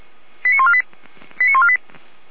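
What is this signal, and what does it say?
Five-tone selective call (ZVEI), heard as quick runs of five tones, tones of about 70 ms. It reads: 89148 89148
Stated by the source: ZVEI five-tone selective call received on an SDR.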